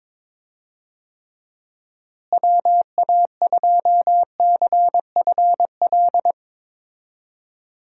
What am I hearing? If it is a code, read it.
Morse code, heard as a keyed tone, about 21 wpm, 700 Hz. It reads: WA2CFL
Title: USA Call Signs - 6 Characters - Repeated 3x - 22wpm